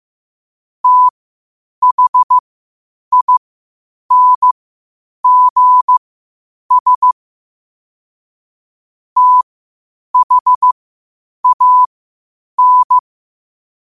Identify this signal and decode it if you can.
Morse code, heard as a keyed tone, about 15 wpm, spaced as for 8 wpm, 1000 Hz. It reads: THINGS THAN
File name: qotdfiles / 8wpm